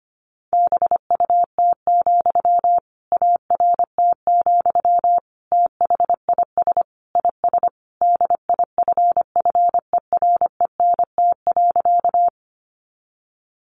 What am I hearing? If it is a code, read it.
Morse code, heard as a keyed tone, about 25 wpm, 700 Hz. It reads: BUT, ART, THIS IS DIFFERENT.
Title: Sentences from Top 600 Words 25wpm